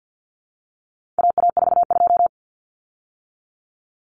Morse code, heard as a keyed tone, about 50 wpm, 700 Hz. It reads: AA42
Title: Common Cut Numbers - 4-Digit 50wpm